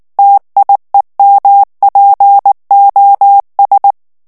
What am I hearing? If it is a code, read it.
Morse code, heard as a keyed tone, about 19 wpm, 800 Hz. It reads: TIEMPOS
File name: morse mp3 / ESPAGNOL TEXTE EN CLAIR 2 VITESSE 18 MOTS MINUTES